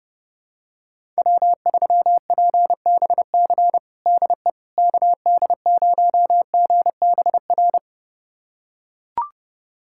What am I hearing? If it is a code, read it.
Morse code, heard as a keyed tone, about 30 wpm, 700 Hz. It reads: W3PBC DE KD0GBR
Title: USA Callsign DE Callsign 30wpm